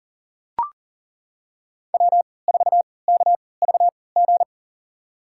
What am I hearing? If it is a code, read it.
Morse code, heard as a keyed tone, about 40 wpm, spaced as for 23 wpm, 700 Hz. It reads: W4KVG
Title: USA Call Signs - 5 Characters - 3x Character Spacing - 40wpm